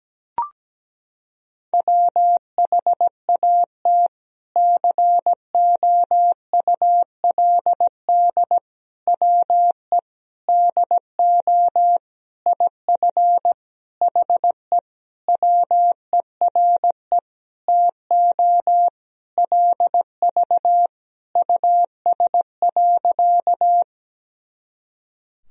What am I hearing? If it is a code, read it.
Morse code, heard as a keyed tone, about 17 wpm, 700 Hz. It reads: WHAT COULD WE DO IF HE WERE TO LV US.
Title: Sentences from Top 400 Words and Abbreviations 17wpm